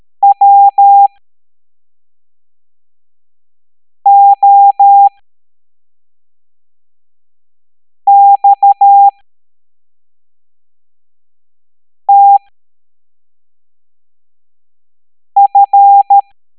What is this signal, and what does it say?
Morse code, heard as a keyed tone, about 13 wpm, spaced as for 3 wpm, 800 Hz. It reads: WOXTF